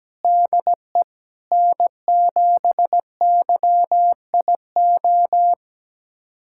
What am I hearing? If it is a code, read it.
Morse code, heard as a keyed tone, about 17 wpm, 700 Hz. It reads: DE N7YIO
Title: USA Callsign DE Callsign - 5 Characters - 17wpm